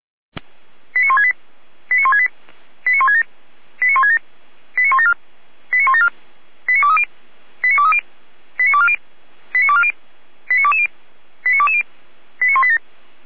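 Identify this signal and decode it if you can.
Five-tone selective call (ZVEI), heard as quick runs of five tones, tones of about 70 ms. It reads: 89168 89168 89167 89167 89174 89174 89230 89230 89240 89240 89220 89220 89187
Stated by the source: ZVEI five-tone selective call received on an SDR.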